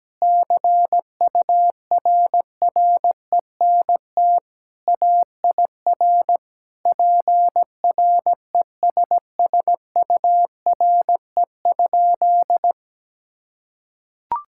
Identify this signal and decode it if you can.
Morse code, heard as a keyed tone, about 17 wpm, 700 Hz. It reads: CURRENT AIR PRESSURE?
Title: Sentences from Top 1000 Words - Review Part 6 - 17wpm